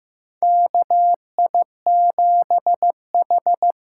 Morse code, heard as a keyed tone, about 15 wpm, 700 Hz. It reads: KI7H
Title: USA Call Signs - 6 Characters - 6x Character Spacing - 15wpm